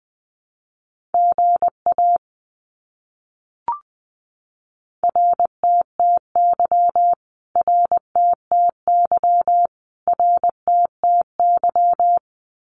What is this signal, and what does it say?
Morse code, heard as a keyed tone, about 20 wpm, 700 Hz. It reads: GA RTTY RTTY RTTY
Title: Abbreviations Repeated 3x - 20wpm